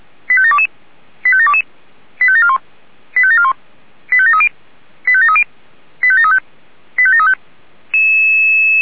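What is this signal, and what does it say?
Five-tone selective call (ZVEI), heard as quick runs of five tones, tones of about 70 ms. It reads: 85722 85722 85741 85741 85730 85730 85735 85735
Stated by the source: ZVEI five-tone selective call 85825 received with an SDR.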